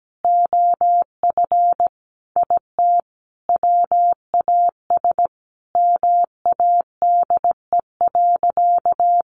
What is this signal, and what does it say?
Morse code, heard as a keyed tone, about 17 wpm, 700 Hz. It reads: OF IT WAS MADE.
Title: Sentences from Top 1400 Words 17wpm